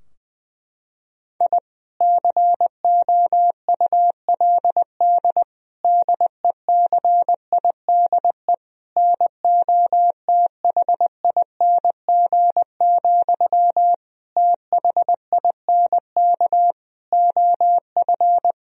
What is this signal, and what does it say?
Morse code, heard as a keyed tone, about 21 wpm, 700 Hz. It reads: I COULD DECIDE NOTHING, THINK OF
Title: Sentences from Top 1000 Words - Review Part 3 - 20wpm